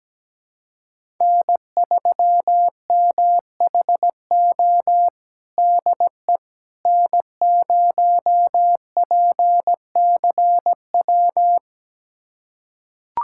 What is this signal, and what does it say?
Morse code, heard as a keyed tone, about 17 wpm, 700 Hz. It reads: N3MHO DE N0PCW
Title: USA Callsign DE Callsign - 5 Characters - 17wpm